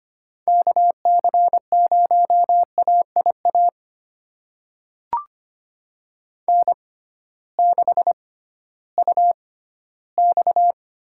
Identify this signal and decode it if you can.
Morse code, heard as a keyed tone, about 25 wpm, spaced as for 14 wpm, 700 Hz. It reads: KC0AIA N6UX
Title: USA Call Signs - 6x Character Spacing - 25wpm